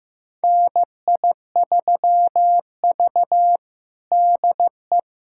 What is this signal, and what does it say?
Morse code, heard as a keyed tone, about 15 wpm, 700 Hz. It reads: NI3V DE
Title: USA Callsign DE Callsign - 4 Characters - 15wpm